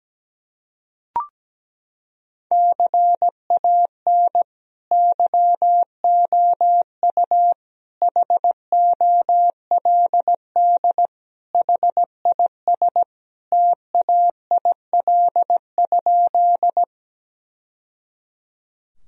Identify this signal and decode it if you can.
Morse code, heard as a keyed tone, about 17 wpm, 700 Hz. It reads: CAN YOU HOLD HIS TAIL?
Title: Sentences from Top 1000 Words - Review Part 3 - 17wpm